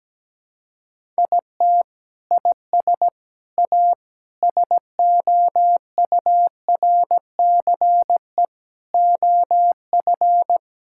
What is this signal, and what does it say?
Morse code, heard as a keyed tone, about 17 wpm, 700 Hz. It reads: IT IS A SOURCE OF
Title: Sentences from Top 1000 Words - Review Part 5 - 17wpm